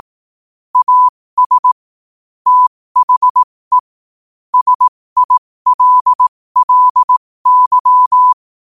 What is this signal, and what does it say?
Morse code, heard as a keyed tone, about 18 wpm, spaced as for 15 wpm, 1000 Hz. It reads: AS THE SILLY
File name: qotdfiles / 15wpm